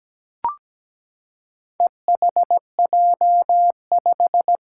Morse code, heard as a keyed tone, about 17 wpm, 700 Hz. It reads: EHJ5
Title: Four Challenging Letter-Number Combinations 17wpm